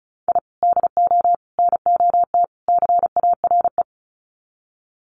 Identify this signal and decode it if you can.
Morse code, heard as a keyed tone, about 34 wpm, 700 Hz. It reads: I DO NOT CARE
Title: Sets of 4 Words 35wpm